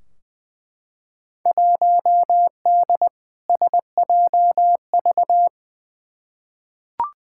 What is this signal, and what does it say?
Morse code, heard as a keyed tone, about 20 wpm, 700 Hz. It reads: 1D SJV
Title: Field Day Exchanges Encore - 20wpm